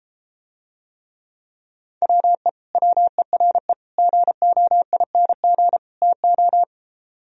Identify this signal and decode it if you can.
Morse code, heard as a keyed tone, about 33 wpm, 700 Hz. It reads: WE WERE GOING TO